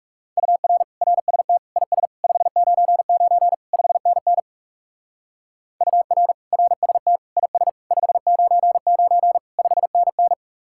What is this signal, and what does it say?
Morse code, heard as a keyed tone, about 39 wpm, 700 Hz. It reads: UR RST IS 599 5NN UR RST IS 599 5NN